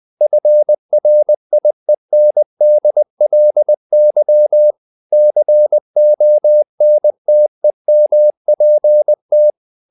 Morse code, heard as a keyed tone, about 20 wpm, 600 Hz. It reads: FRIENDLY CONTEMPT
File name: the old man and the sea / the old man and the sea part 1 20wpm